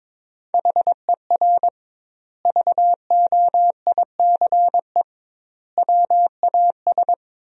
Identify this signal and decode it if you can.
Morse code, heard as a keyed tone, about 22 wpm, 700 Hz. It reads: HER VOICE WAS
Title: 2x Word-Spacing - Sentences from Top 500 Words - 22wpm